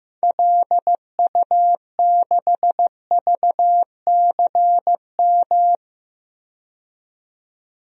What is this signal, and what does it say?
Morse code, heard as a keyed tone, about 15 wpm, 700 Hz. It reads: LU6VCM
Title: DXCC-100 Call Signs - 6 Characters - 15wpm